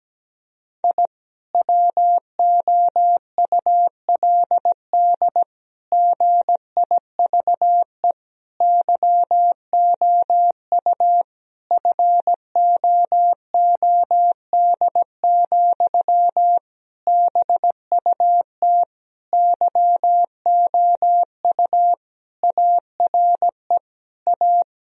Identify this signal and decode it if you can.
Morse code, heard as a keyed tone, about 17 wpm, 700 Hz. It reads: I WOULD GIVE YOU FOOD, BUT YOU ARE A
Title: Sentences from Top 1000 Words - Review Part 1 - 17wpm